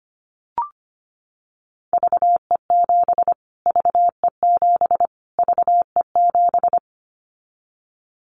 Morse code, heard as a keyed tone, about 25 wpm, 700 Hz. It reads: VE7 VE7 VE7
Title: Callsign Prefixes for Top 25 DXCC - Repeated 3x - 25wpm